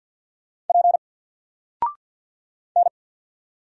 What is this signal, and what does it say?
Morse code, heard as a keyed tone, about 50 wpm, 700 Hz. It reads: P N
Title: Single Character (Letter:Number:Punctuation:Prosign) 50wpm